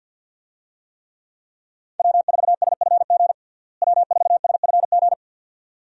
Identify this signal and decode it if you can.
Morse code, heard as a keyed tone, about 48 wpm, 700 Hz. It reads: W4SFG W4SFG
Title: USA Call Signs - 5 Characters - Repeated 2x - 50wpm